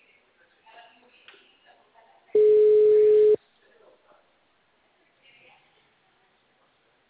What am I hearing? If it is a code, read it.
Ringback tone (Europe, 425 Hz), with speech.